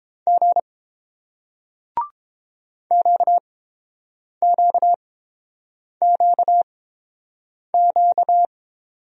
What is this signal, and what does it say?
Morse code, heard as a keyed tone, about 30 wpm, spaced as for 7 wpm, 700 Hz. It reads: G QQQQ